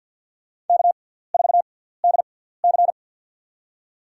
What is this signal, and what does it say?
Morse code, heard as a keyed tone, about 53 wpm, spaced as for 17 wpm, 690 Hz. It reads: K4DC
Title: USA Call Signs - 4 Characters - 6x Character Spacing - 50wpm